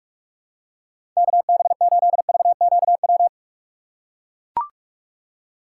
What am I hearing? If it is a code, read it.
Morse code, heard as a keyed tone, about 44 wpm, 700 Hz. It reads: KB8VQW
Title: USA Call Signs - 2x Character Spacing - 45wpm